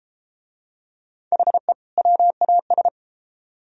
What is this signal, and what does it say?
Morse code, heard as a keyed tone, about 31 wpm, 700 Hz. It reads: HE WAS